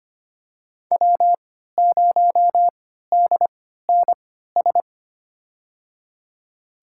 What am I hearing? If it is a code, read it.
Morse code, heard as a keyed tone, about 25 wpm, spaced as for 14 wpm, 700 Hz. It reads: W0DNS
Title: USA Call Signs - 5 Characters - 3x Character Spacing - 25wpm